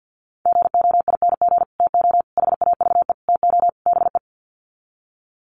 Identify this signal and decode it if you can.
Morse code, heard as a keyed tone, about 50 wpm, 700 Hz. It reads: GOING TO HAVE TO BE
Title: Sets of 5 Words 50wpm